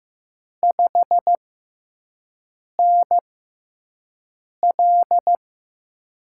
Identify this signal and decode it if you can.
Morse code, heard as a keyed tone, about 15 wpm, spaced as for 5 wpm, 700 Hz. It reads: 5NL
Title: USA Call Signs - 5 Characters - 6x Character Spacing - 15wpm